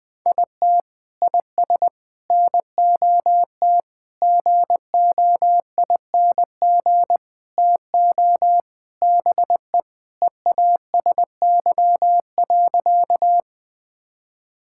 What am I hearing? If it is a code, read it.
Morse code, heard as a keyed tone, about 20 wpm, 700 Hz. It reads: IT IS NOT GOING TO BE EASY.
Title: Sentences with 4-Word Phrases - 20wpm